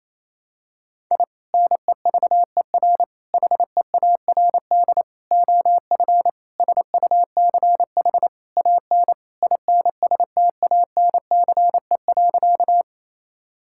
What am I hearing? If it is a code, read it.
Morse code, heard as a keyed tone, about 27 wpm, 700 Hz. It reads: I NEVER HEARD OF SUCH AN INSTANCE.
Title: Sentences from Top 1000 Words - Review Part 5 - 28wpm